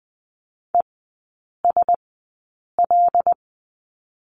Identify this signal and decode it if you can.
Morse code, heard as a keyed tone, about 20 wpm, spaced as for 8 wpm, 700 Hz. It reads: ESL
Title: 152 Lesson 25e - Letter p - Three Characters 20wpm